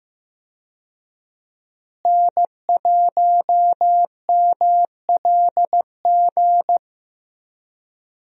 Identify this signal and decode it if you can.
Morse code, heard as a keyed tone, about 15 wpm, 700 Hz. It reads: N1MLG
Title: USA Call Signs - 5 Characters - Repeated 2x - 15wpm